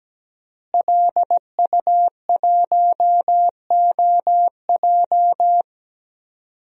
Morse code, heard as a keyed tone, about 17 wpm, 700 Hz. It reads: LU1OJ
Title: DXCC-100 Call Signs - 5 Characters - Repeated 2x - 17wpm